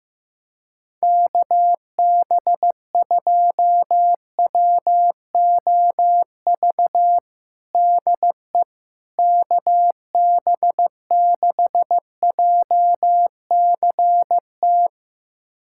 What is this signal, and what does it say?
Morse code, heard as a keyed tone, about 15 wpm, 700 Hz. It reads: KB2WOV DE KB6JCT
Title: USA Callsign DE Callsign - 6 Characters - 15wpm